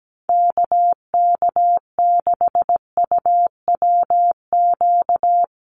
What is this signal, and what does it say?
Morse code, heard as a keyed tone, about 17 wpm, 700 Hz. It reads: KK6UWQ
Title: USA Call Signs 17wpm